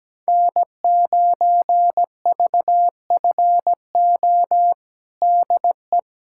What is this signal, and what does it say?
Morse code, heard as a keyed tone, about 17 wpm, 700 Hz. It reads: N9VFO DE
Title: USA Callsign DE Callsign 17wpm